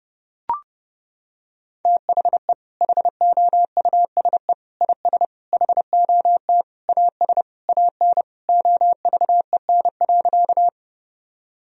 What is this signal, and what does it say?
Morse code, heard as a keyed tone, about 32 wpm, 700 Hz. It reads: THE HOUSE IS HOT AS AN OVEN.
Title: Sentences from Top 1000 Words - Review Part 4 - 30wpm